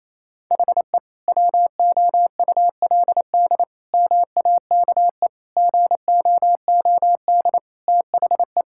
Morse code, heard as a keyed tone, about 28 wpm, 700 Hz. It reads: HE WOULD MAKE GOOD THE
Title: Sentences from Top 1600 Words 28wpm